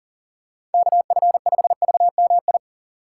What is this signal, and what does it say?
Morse code, heard as a keyed tone, about 40 wpm, 700 Hz. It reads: KF5VMI
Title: USA Call Signs - 6 Characters - 3x Character Spacing - 40wpm